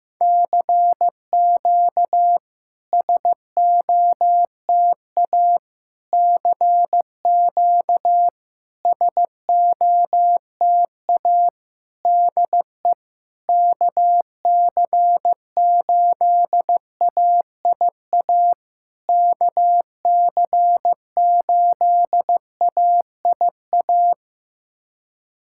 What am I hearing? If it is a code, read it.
Morse code, heard as a keyed tone, about 15 wpm, 690 Hz. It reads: CQ SOTA CQ SOTA DE KC8AIA KC8AIA
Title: SOTA QSO Practice 15wpm